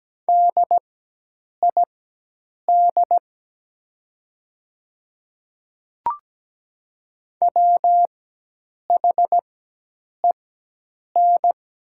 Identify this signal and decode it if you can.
Morse code, heard as a keyed tone, about 17 wpm, spaced as for 8 wpm, 700 Hz. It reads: DID WHEN
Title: 4x Character Spacing - Top 100 Words - 17wpm